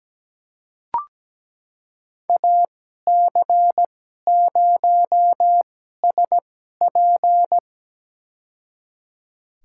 Morse code, heard as a keyed tone, about 17 wpm, spaced as for 12 wpm, 700 Hz. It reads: AC0SP
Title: USA Call Signs - 5 Characters - 2x Character Spacing - 17wpm